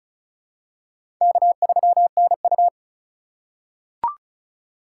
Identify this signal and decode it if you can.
Morse code, heard as a keyed tone, about 36 wpm, 700 Hz. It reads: K3NU
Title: USA Call Signs - 8x Character Spacing - 35wpm